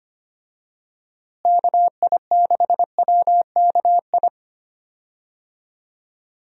Morse code, heard as a keyed tone, about 25 wpm, 700 Hz. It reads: KI6WKI
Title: USA Call Signs - 6 Characters - Repeated 3x - 25wpm